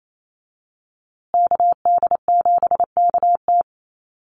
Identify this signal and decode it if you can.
Morse code, heard as a keyed tone, about 28 wpm, 700 Hz. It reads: KD7KT
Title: USA Call Signs Repeated 2x 28wpm